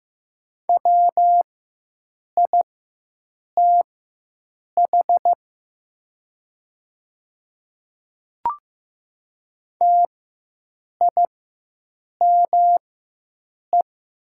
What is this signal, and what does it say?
Morse code, heard as a keyed tone, about 15 wpm, spaced as for 7 wpm, 700 Hz. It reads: WITH TIME
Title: 4x Character Spacing - Top 200 Words - 15wpm